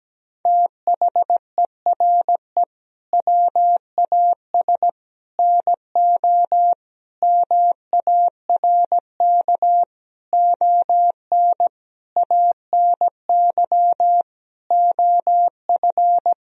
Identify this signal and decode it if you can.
Morse code, heard as a keyed tone, about 17 wpm, 700 Hz. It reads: THERE WAS NO MARK ON ANY OF
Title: Sentences from Top 1000 Words - Review Part 1 - 17wpm